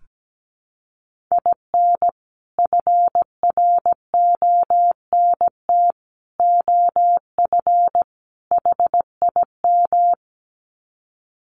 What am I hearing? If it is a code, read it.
Morse code, heard as a keyed tone, about 17 wpm, 700 Hz. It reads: IN FRONT OF HIM
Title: Sets of 4 Words 17wpm